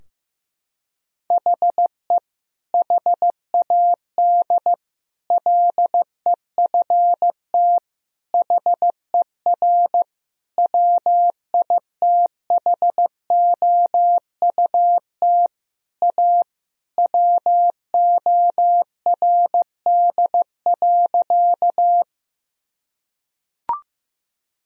Morse code, heard as a keyed tone, about 15 wpm, 700 Hz. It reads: HE HAD LEFT HER WITHOUT A WORD.
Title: Sentences from Top 1000 Words - Review Part 3 - 15wpm